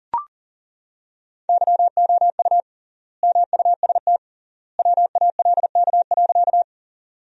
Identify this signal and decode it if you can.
Morse code, heard as a keyed tone, about 37 wpm, 700 Hz. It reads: YOU MUST WALK.
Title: 3x Word-Spacing - Sentences from Top 300 Words - 40wpm